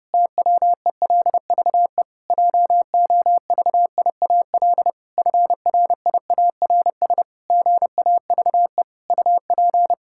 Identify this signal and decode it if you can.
Morse code, heard as a keyed tone, about 28 wpm, 700 Hz. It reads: TWELVE JOVIAL FRIARS GAVE UP
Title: Pangram Challenge v2 - 30wpm